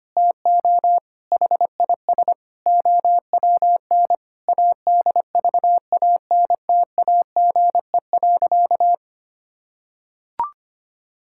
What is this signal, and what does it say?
Morse code, heard as a keyed tone, about 25 wpm, 700 Hz. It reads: TO HIS OWN ADVANTAGE.
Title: Sentences from Top 1000 Words - Review Part 6 - 25wpm